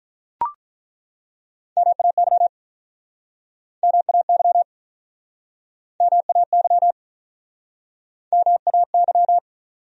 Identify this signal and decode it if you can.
Morse code, heard as a keyed tone, about 48 wpm, 700 Hz. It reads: MAY MAY MAY MAY